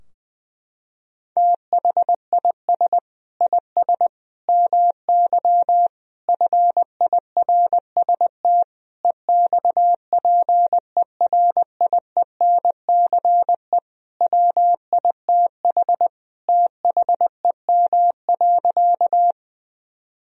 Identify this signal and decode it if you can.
Morse code, heard as a keyed tone, about 20 wpm, 700 Hz. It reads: THIS IS MY FIRST EXPERIENCE WITH THEM.
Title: Sentences from Top 1000 Words - Review Part 3 - 20wpm